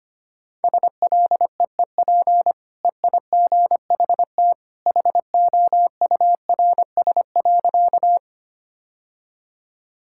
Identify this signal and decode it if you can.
Morse code, heard as a keyed tone, about 25 wpm, 700 Hz. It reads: SLEEP EIGHT HOURS.
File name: Easy Sentences with Hints - 25wpm